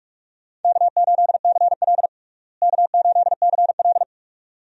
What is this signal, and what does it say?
Morse code, heard as a keyed tone, about 46 wpm, 700 Hz. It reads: K8CL K8CL
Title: USA Call Signs - 4 Characters - Repeated 2x - 45wpm